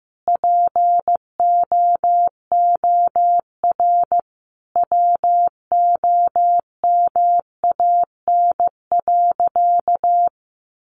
Morse code, heard as a keyed tone, about 15 wpm, 700 Hz. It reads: POOR WOMAN.
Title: Sentences from Top 1100 Words 15wpm